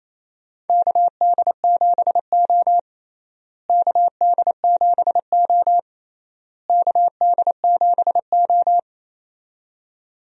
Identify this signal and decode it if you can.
Morse code, heard as a keyed tone, about 28 wpm, 700 Hz. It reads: KD7O KD7O KD7O
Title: USA Call Signs - 4 Characters - Repeated 3x - 28wpm